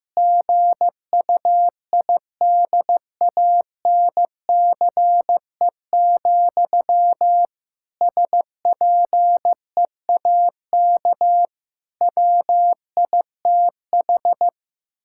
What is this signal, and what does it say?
Morse code, heard as a keyed tone, about 15 wpm, 700 Hz. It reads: GUIDANCE, SPEAK WITH
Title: Sentences from Top 1600 Words 15wpm